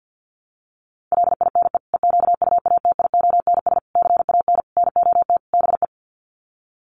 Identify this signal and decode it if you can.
Morse code, heard as a keyed tone, about 50 wpm, 700 Hz. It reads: LINE EQUATIONS CAN NOT BE
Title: Sets of 5 Words 50wpm